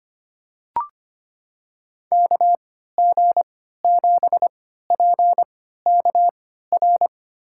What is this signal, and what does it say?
Morse code, heard as a keyed tone, about 25 wpm, spaced as for 14 wpm, 690 Hz. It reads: KG7PKR
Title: USA Call Signs - 6 Characters - 3x Character Spacing - 25wpm